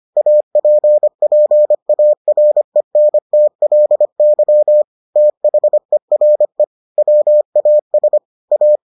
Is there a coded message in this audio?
Yes: Morse code from 0.2 to 8.8 s, heard as a keyed tone, about 25 wpm, 600 Hz. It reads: APPARENTLY THERE WAS A